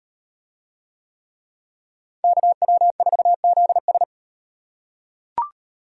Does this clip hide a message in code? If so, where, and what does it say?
Yes: Morse code from 2.2 to 4.0 s, heard as a keyed tone, about 37 wpm, 700 Hz. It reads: KW4ZS